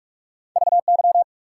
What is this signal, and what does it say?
Morse code, heard as a keyed tone, about 46 wpm, 700 Hz. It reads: VY
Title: Top 90 QSO Elements 45wpm